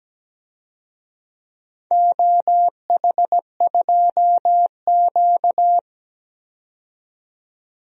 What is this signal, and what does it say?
Morse code, heard as a keyed tone, about 17 wpm, 700 Hz. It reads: OH2Q
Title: DXCC-100 Call Signs - 4 Characters - 17wpm